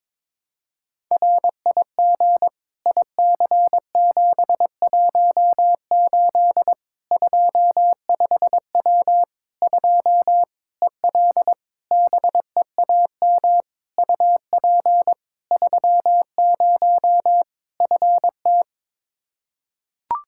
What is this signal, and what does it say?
Morse code, heard as a keyed tone, about 22 wpm, 700 Hz. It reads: RIG IC718 25W 2 EL BEAM UP 30 FT
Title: Practice QSO - Exchange 2 - Info - 22wpm